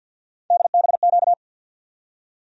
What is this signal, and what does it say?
Morse code, heard as a keyed tone, about 53 wpm, 700 Hz. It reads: DBQ
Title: USA Call Sign Postfixes - 3 Letters 50wpm